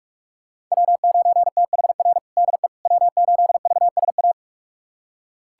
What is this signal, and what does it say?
Morse code, heard as a keyed tone, about 46 wpm, 700 Hz. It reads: W9THR DE W8VSA
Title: USA Callsign DE Callsign - 5 Characters - 45wpm